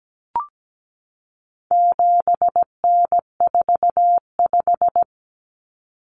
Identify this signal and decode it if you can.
Morse code, heard as a keyed tone, about 17 wpm, 700 Hz. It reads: 7N45
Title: Common Cut Numbers - 4-Digit 17wpm